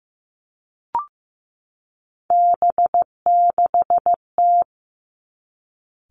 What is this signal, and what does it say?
Morse code, heard as a keyed tone, about 15 wpm, 700 Hz. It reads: B6T
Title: Three Challenging Letter-Number Combinations 15wpm